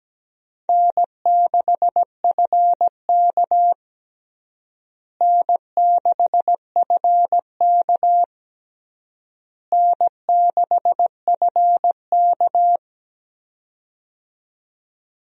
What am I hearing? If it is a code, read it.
Morse code, heard as a keyed tone, about 17 wpm, 700 Hz. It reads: N6FK N6FK N6FK
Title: USA Call Signs - 4 Characters - Repeated 3x - 17wpm